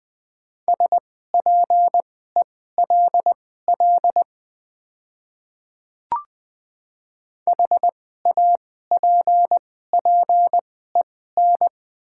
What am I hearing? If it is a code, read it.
Morse code, heard as a keyed tone, about 20 wpm, spaced as for 14 wpm, 690 Hz. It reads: SPELL HAPPEN
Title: 2x Character Spacing - Top 500 Words - 20wpm